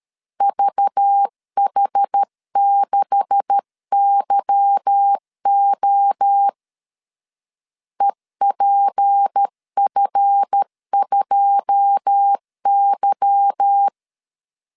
Morse code, heard as a keyed tone, about 13 wpm, 800 Hz. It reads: VH6YO EPF2Y
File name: LEZIONI / 9 286 12wpm 2305